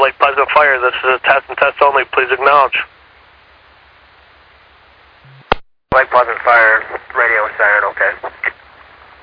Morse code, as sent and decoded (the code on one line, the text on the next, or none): none